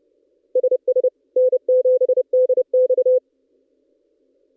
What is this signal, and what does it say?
Morse code, heard as a keyed tone, about 30 wpm, 510 Hz. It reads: SS N7DX
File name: contests / ss-cw-k5tr-op-n5rz -1700